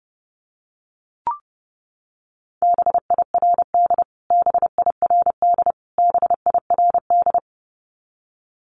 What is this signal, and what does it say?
Morse code, heard as a keyed tone, about 30 wpm, 700 Hz. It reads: BIRD BIRD BIRD
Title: Top 500 Words Repeated 30wpm